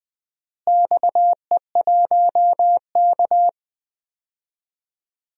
Morse code, heard as a keyed tone, about 20 wpm, 700 Hz. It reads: XE1K
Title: DXCC-100 Call Signs - 4 Characters - Repeated 2x - 20wpm